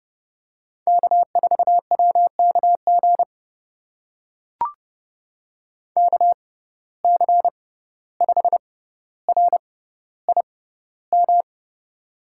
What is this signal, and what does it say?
Morse code, heard as a keyed tone, about 30 wpm, spaced as for 15 wpm, 700 Hz. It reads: K4WKG KC5RIM
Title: USA Call Signs - 6x Character Spacing - 30wpm